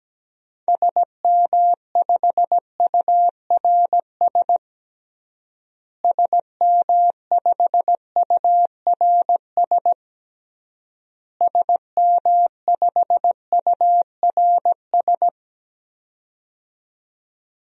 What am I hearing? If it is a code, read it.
Morse code, heard as a keyed tone, about 17 wpm, 700 Hz. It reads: SM5URS SM5URS SM5URS
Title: DXCC-100 Call Signs - 6 Characters - Repeated 3x - 17wpm